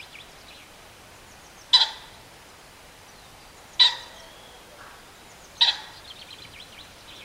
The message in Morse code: none